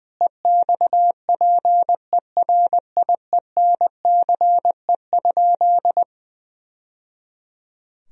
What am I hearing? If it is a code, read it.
Morse code, heard as a keyed tone, about 20 wpm, 700 Hz. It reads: EXPERIENCE?
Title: Sentences from Top 1000 Words - Review Part 3 - 20wpm